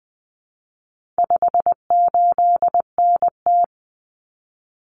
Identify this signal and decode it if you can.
Morse code, heard as a keyed tone, about 20 wpm, 700 Hz. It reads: 58NT
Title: Common Cut Numbers - 4-Digit 20wpm